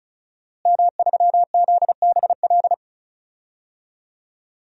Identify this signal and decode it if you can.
Morse code, heard as a keyed tone, about 34 wpm, 700 Hz. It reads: M3ZBL
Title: DXCC-100 Call Signs - 5 Characters - Repeated 2x - 35wpm